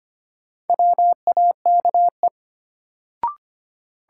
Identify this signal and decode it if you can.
Morse code, heard as a keyed tone, about 25 wpm, 700 Hz. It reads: WAKE
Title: Sets of 4 Words - New Word Drill - 25wpm